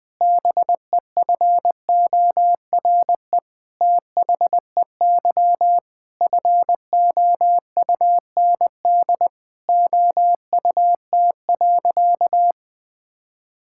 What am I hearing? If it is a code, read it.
Morse code, heard as a keyed tone, about 20 wpm, 700 Hz. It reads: BEFORE THEY FOUND OUT.
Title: Sentences with 8-Word Phrases - Spoken Beginning - 20wpm